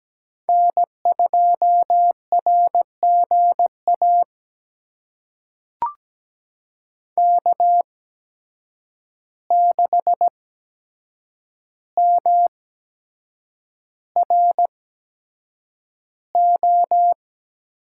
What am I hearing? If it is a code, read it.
Morse code, heard as a keyed tone, about 17 wpm, spaced as for 7 wpm, 700 Hz. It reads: N2RGA K6MRO
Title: USA Call Signs - 8x Character Spacing - 17wpm